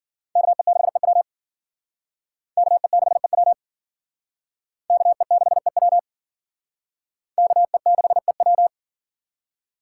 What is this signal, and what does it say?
Morse code, heard as a keyed tone, about 51 wpm, 700 Hz. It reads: KE6EW KE6EW KE6EW KE6EW